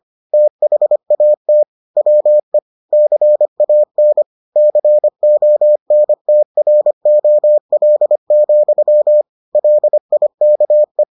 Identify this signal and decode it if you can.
Morse code, heard as a keyed tone, about 25 wpm, 600 Hz. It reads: THAT WE CAN CONTROL, LIKE